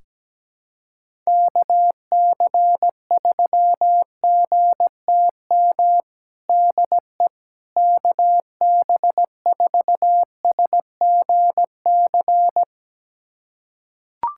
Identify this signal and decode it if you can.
Morse code, heard as a keyed tone, about 17 wpm, 700 Hz. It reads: KC3GTM DE KB4SGC
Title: USA Callsign DE Callsign - 6 Characters - 17wpm